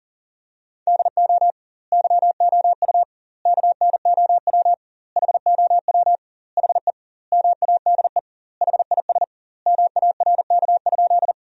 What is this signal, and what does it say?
Morse code, heard as a keyed tone, about 40 wpm, 700 Hz. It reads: DO YOU KNOW HOW HE MADE HIS MARK?
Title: 2x Word-Spacing - Sentences from Top 300 Words - 40wpm